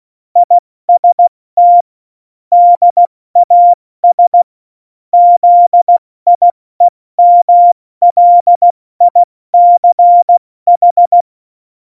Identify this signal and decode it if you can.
Morse code, heard as a keyed tone, about 16 wpm, 700 Hz. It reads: IST DAS ZIEMLICH
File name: Morsekurs / Witze / Witz 19 Mai, 2020 70 F80